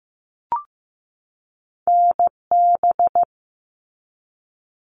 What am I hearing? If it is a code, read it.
Morse code, heard as a keyed tone, about 15 wpm, 700 Hz. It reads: NB
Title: Canadian Provinces and Territories 15wpm